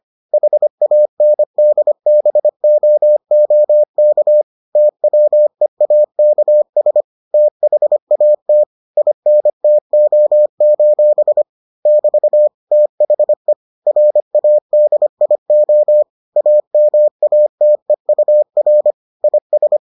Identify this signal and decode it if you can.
Morse code, heard as a keyed tone, about 25 wpm, 600 Hz. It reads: HANDBOOK TWEAKS THAT INTO: <BT> THE RADIO AMATEUR IS